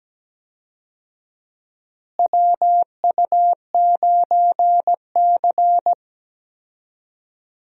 Morse code, heard as a keyed tone, about 17 wpm, 700 Hz. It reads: WU9C
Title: USA Call Signs - 4 Characters - Repeated 2x - 17wpm